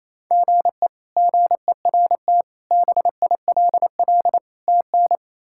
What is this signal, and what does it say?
Morse code, heard as a keyed tone, about 27 wpm, 700 Hz. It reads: GE GERT BILL TN
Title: K1USN SST QSO Practice - 28wpm